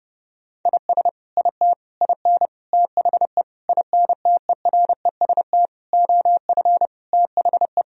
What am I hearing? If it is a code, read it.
Morse code, heard as a keyed tone, about 30 wpm, 700 Hz. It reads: IS IT IN THE INTEREST OF THE
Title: Sentences from Top 1000 Words - Review Part 4 - 30wpm